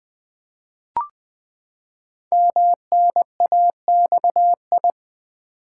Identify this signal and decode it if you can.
Morse code, heard as a keyed tone, about 20 wpm, 700 Hz. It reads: MNAXI
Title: Code Groups - 5 Letters - 20wpm